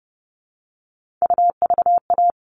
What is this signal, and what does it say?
Morse code, heard as a keyed tone, about 30 wpm, 700 Hz. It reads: UVA
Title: All Cut Numbers - 3-Digit 30wpm